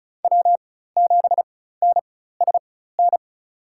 Morse code, heard as a keyed tone, about 34 wpm, spaced as for 16 wpm, 700 Hz. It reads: W7NSN
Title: USA Call Signs - 4x Character Spacing - 35wpm